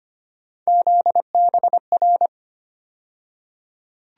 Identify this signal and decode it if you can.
Morse code, heard as a keyed tone, about 25 wpm, 700 Hz. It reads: ZBR
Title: USA Call Sign Postfixes - 3 Letters 25wpm